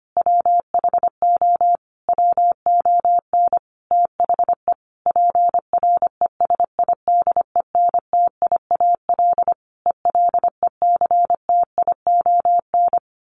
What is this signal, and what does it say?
Morse code, heard as a keyed tone, about 25 wpm, 700 Hz. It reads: WHO WON THE PRESIDENTIAL ELECTION
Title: A Sound Of Thunder 25wpm